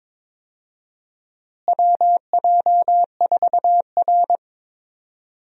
Morse code, heard as a keyed tone, about 22 wpm, 700 Hz. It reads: WJ4R
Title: USA Call Signs - 4 Characters - Repeated 3x - 22wpm